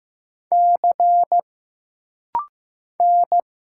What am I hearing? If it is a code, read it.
Morse code, heard as a keyed tone, about 15 wpm, 700 Hz. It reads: C N
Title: Single Letters - Rapid-Fire 15wpm